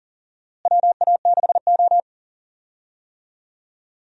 Morse code, heard as a keyed tone, about 40 wpm, 700 Hz. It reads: WA6O
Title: USA Call Signs - 4 Characters - Repeated 2x - 40wpm